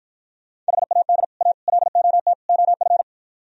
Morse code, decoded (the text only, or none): HAD A LOT OF